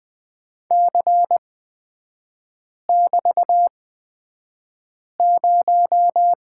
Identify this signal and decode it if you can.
Morse code, heard as a keyed tone, about 20 wpm, spaced as for 5 wpm, 700 Hz. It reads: C<BT>0
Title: 255 Lesson 41e - Period - Three Characters 20wpm